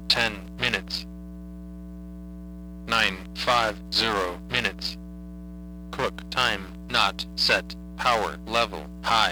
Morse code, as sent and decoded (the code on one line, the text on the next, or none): none